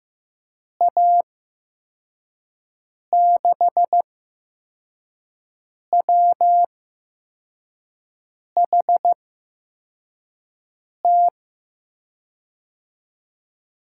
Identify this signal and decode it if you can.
Morse code, heard as a keyed tone, about 15 wpm, spaced as for 4 wpm, 700 Hz. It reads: A6WHT
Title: USA Call Signs - 6 Characters - 8x Character Spacing - 15wpm